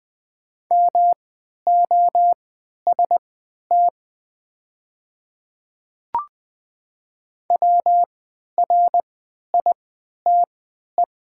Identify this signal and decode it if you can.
Morse code, heard as a keyed tone, about 20 wpm, spaced as for 11 wpm, 700 Hz. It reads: MOST WRITE
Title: 3x Character Spacing - Top 100 Words - 20wpm